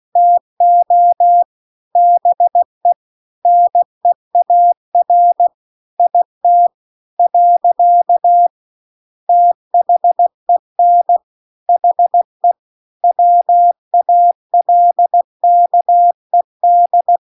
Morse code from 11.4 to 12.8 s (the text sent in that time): HE